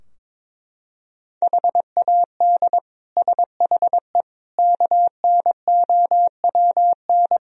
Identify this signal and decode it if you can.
Morse code, heard as a keyed tone, about 22 wpm, 700 Hz. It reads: HAD SHE KNOWN
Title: Sentences from Top 1000 Words - Review Part 3 - 22wpm